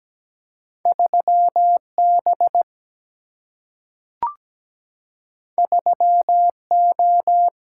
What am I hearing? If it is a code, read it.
Morse code, heard as a keyed tone, about 17 wpm, 700 Hz. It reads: 3B 3O
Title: Single-Letter Single-Number 17wpm